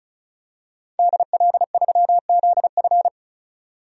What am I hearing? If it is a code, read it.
Morse code, heard as a keyed tone, about 34 wpm, 700 Hz. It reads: DL3ZF
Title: DXCC-100 Call Signs - 5 Characters - 35wpm